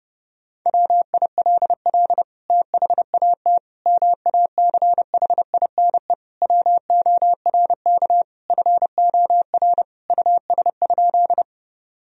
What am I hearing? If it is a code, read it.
Morse code, heard as a keyed tone, about 29 wpm, 700 Hz. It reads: WILL THAT MACHINE WORK FOR US?
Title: Sentences from Top 1000 Words - Review Part 2 - 30wpm